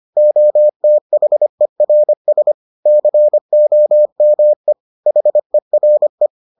Morse code, heard as a keyed tone, about 25 wpm, 600 Hz. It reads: OTHERS COME HERE